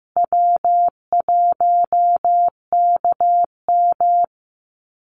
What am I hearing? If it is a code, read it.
Morse code, heard as a keyed tone, about 15 wpm, 700 Hz. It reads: W1KM
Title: Call Signs Repeated 3x - 15wpm